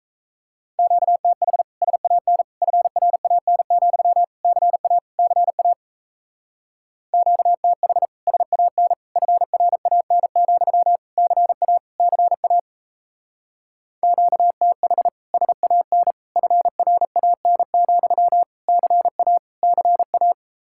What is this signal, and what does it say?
Morse code, heard as a keyed tone, about 38 wpm, 700 Hz. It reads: QTH SAN FRAN, CA CA QTH SAN FRAN, CA CA QTH SAN FRAN, CA CA